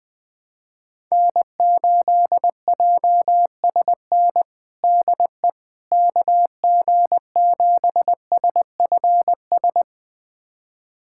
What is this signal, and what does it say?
Morse code, heard as a keyed tone, about 21 wpm, 700 Hz. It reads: N8JSN DE KG7SFS
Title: USA Callsign DE Callsign 20wpm